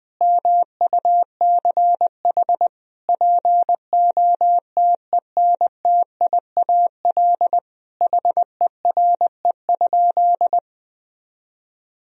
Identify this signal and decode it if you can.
Morse code, heard as a keyed tone, about 20 wpm, 700 Hz. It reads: MUCH POTENTIAL HERE?
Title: Sentences from Top 1000 Words - Review Part 4 - 20wpm